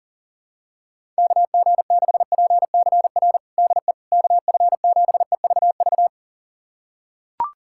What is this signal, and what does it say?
Morse code, heard as a keyed tone, about 40 wpm, 700 Hz. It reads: KG6PCR DE KF7EVV
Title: USA Callsign DE Callsign - 6 Characters - 40wpm